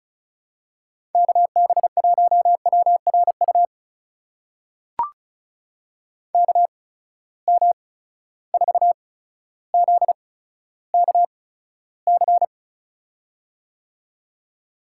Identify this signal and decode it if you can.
Morse code, heard as a keyed tone, about 36 wpm, spaced as for 15 wpm, 700 Hz. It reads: KB1WRU KM4ZKC
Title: USA Call Signs - 6 Characters - 8x Character Spacing - 35wpm